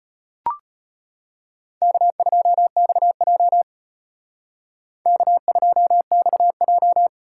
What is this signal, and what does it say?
Morse code, heard as a keyed tone, about 33 wpm, 700 Hz. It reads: K2XJ K2XJ